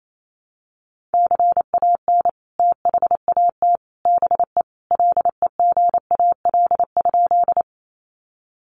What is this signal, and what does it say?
Morse code, heard as a keyed tone, about 28 wpm, 700 Hz. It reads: CAN THAT BE LEGAL?
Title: Sentences from Top 1200 Words 28wpm